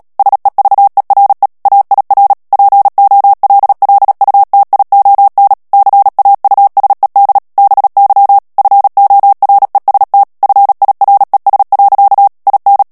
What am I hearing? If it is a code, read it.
Morse code, heard as a keyed tone, about 37 wpm, 800 Hz. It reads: SEVERE AIR POLLUTION CAUSED BY FOREST FIRES. IN
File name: morse mp3 / ANGLAIS texte clair 2 vitesse 35 mots minute